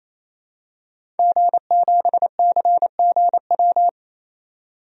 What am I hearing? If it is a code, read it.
Morse code, heard as a keyed tone, about 28 wpm, 700 Hz. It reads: G7CGW
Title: DXCC-100 Call Signs - 5 Characters - 28wpm